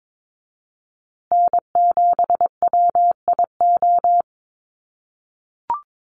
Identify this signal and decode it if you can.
Morse code, heard as a keyed tone, about 22 wpm, 700 Hz. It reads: N7WIO
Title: USA Call Signs Repeated 2x 22wpm